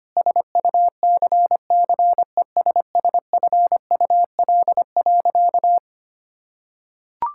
Morse code, heard as a keyed tone, about 25 wpm, 700 Hz. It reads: SUCCESSFUL.
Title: Sentences from Top 1000 Words - Review Part 6 - 25wpm